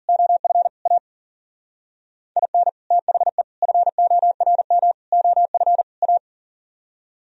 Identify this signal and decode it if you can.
Morse code, heard as a keyed tone, about 52 wpm, 700 Hz. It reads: OF A IN THE FORM OF A